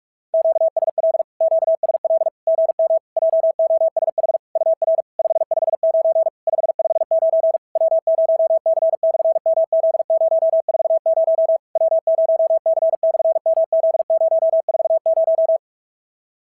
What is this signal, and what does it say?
Morse code, heard as a keyed tone, about 46 wpm, 690 Hz. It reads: QSL QSL GM JOSH UR 559 559 W0C/MZ040 W0C/MZ040
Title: SOTA QSO Practice 45wpm